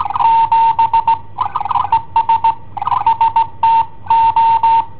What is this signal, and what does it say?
Morse code, heard as a keyed tone, about 18 wpm, 910 Hz. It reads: 7 ES ST O
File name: i7ohp / cassetta i7ohp B